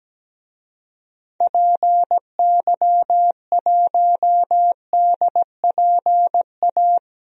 Morse code, heard as a keyed tone, about 17 wpm, 700 Hz. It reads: PY1DPA
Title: DXCC-100 Call Signs - 6 Characters - Repeated 2x - 17wpm